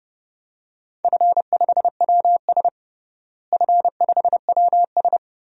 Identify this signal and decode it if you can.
Morse code, heard as a keyed tone, about 30 wpm, 700 Hz. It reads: F5WS F5WS
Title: DXCC-100 Call Signs - 4 Characters - Repeated 3x - 30wpm